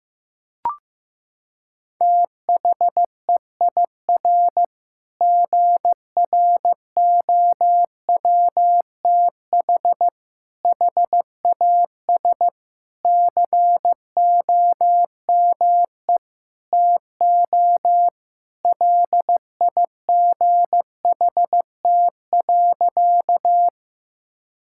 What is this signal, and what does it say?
Morse code, heard as a keyed tone, about 16 wpm, 700 Hz. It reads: THEIR GROWTH HAS COME TO LIGHT.
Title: Sentences from Top 1000 Words - Review Part 5 - 15wpm